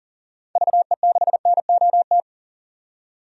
Morse code, heard as a keyed tone, about 40 wpm, 700 Hz. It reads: VE6NOT
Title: DXCC-100 Call Signs - 6 Characters - 40wpm